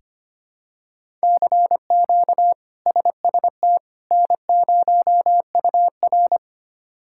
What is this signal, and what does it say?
Morse code, heard as a keyed tone, about 25 wpm, 700 Hz. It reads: CQ SST N0UR
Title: K1USN SST QSO Practice - 25wpm